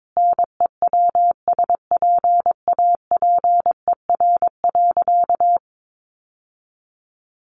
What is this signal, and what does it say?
Morse code, heard as a keyed tone, about 22 wpm, 700 Hz. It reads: NEWSPAPER.
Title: Sentences from Top 1400 Words 22wpm